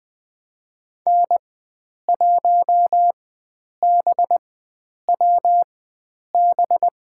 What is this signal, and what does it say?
Morse code, heard as a keyed tone, about 20 wpm, spaced as for 9 wpm, 700 Hz. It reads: N1BWB
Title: USA Call Signs - 5 Characters - 4x Character Spacing - 20wpm